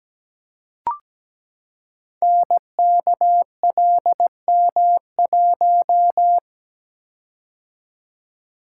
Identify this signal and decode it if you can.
Morse code, heard as a keyed tone, about 17 wpm, 700 Hz. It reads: NKLM1
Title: Code Groups - 5 Letter:Numbers - 17wpm